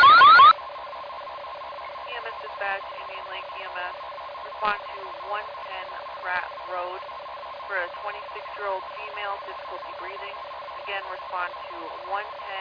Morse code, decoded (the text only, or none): none